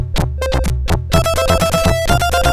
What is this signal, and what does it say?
There is no code.